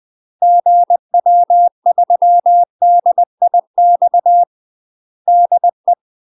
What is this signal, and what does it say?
Morse code, heard as a keyed tone, about 20 wpm, 700 Hz. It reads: GW3DIX DE
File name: UK MASTER QSOs TRAINING / 20WPM / UK MASTER QSOs 0028